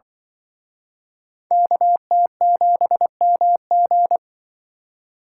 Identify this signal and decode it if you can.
Morse code, heard as a keyed tone, about 25 wpm, 700 Hz. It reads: KT7MG
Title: Speed Racing - USA 2x2 Call Signs 20wpm